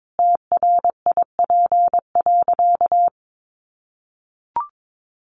Morse code, heard as a keyed tone, about 22 wpm, 700 Hz. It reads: TRIP.
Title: Additional Easy Phrases 22wpm